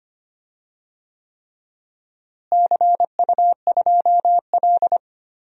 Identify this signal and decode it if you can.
Morse code, heard as a keyed tone, about 25 wpm, 700 Hz. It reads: CU2L
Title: DXCC-100 Call Signs - 4 Characters - Repeated 3x - 25wpm